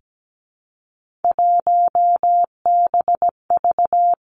Morse code, heard as a keyed tone, about 17 wpm, 700 Hz. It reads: 1BV
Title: Three Challenging Letter-Number Combinations 17wpm